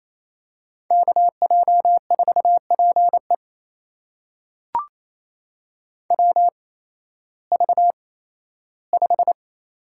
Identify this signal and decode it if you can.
Morse code, heard as a keyed tone, about 28 wpm, spaced as for 15 wpm, 700 Hz. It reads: KJ4PE WV5
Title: USA Call Signs - 8x Character Spacing - 28wpm